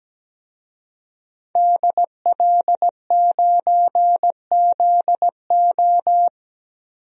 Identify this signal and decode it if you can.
Morse code, heard as a keyed tone, about 17 wpm, 700 Hz. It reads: DL9ZO
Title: DXCC-100 Call Signs - 5 Characters - Repeated 2x - 17wpm